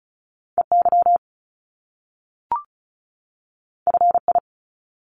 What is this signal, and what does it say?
Morse code, heard as a keyed tone, about 36 wpm, 700 Hz. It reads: EY FI
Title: Top Two Letter Combinations in Words 35wpm